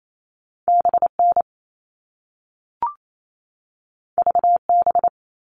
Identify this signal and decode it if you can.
Morse code, heard as a keyed tone, about 28 wpm, 700 Hz. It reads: BN VB
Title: All Cut Numbers - 2-Digit 28wpm